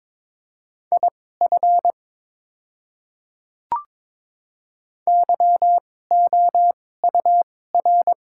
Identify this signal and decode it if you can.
Morse code, heard as a keyed tone, about 22 wpm, spaced as for 16 wpm, 700 Hz. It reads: IF YOUR
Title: 2x Character Spacing - Top 300 Words - 22wpm